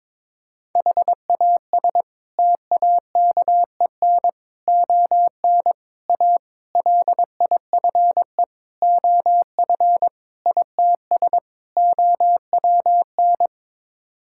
Morse code, heard as a keyed tone, about 22 wpm, 700 Hz. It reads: HAS TAKEN ON A LIFE OF ITS OWN
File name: practice / 22wpm / Sets of 8 Words - 22wpm